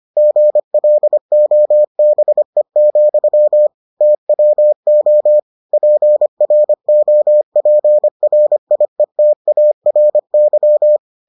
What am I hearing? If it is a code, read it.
Morse code, heard as a keyed tone, about 25 wpm, 600 Hz. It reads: GLOBE, TWO PROPRIETARY